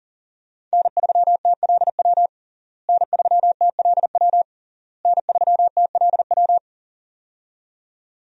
Morse code, heard as a keyed tone, about 40 wpm, 700 Hz. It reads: N3TLW N3TLW N3TLW
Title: USA Call Signs - 5 Characters - Repeated 3x - 40wpm